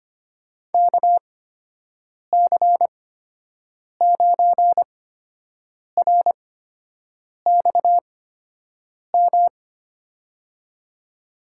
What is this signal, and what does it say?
Morse code, heard as a keyed tone, about 25 wpm, spaced as for 7 wpm, 700 Hz. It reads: KC9RXM
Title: USA Call Signs - 8x Character Spacing - 25wpm